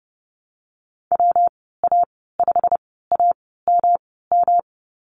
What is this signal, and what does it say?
Morse code, heard as a keyed tone, about 30 wpm, spaced as for 17 wpm, 700 Hz. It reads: WA5AMM
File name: Lesson 19g - Letter w - Call Signs 30wpm